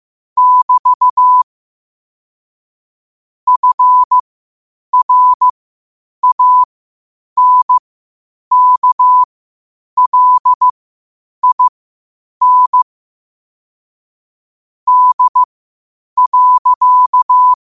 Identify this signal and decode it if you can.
Morse code, heard as a keyed tone, about 15 wpm, spaced as for 8 wpm, 1000 Hz. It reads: <BT> FRANKLIN D.
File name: qotdfiles / 8wpm